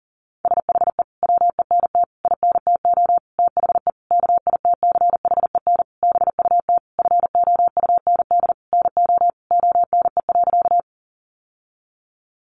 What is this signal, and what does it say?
Morse code, heard as a keyed tone, about 40 wpm, 700 Hz. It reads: SHE WENT INTO THE KITCHEN BUT FOUND NO ONE.
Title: Sentences from Top 1400 Words 40wpm